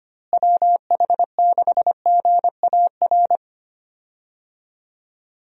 Morse code, heard as a keyed tone, about 25 wpm, 700 Hz. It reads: WH6GAR
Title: USA Call Signs - 6 Characters - Repeated 3x - 25wpm